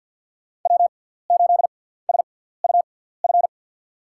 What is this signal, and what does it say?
Morse code, heard as a keyed tone, about 51 wpm, spaced as for 17 wpm, 700 Hz. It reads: W8SUF
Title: USA Call Signs - 5 Characters - 6x Character Spacing - 50wpm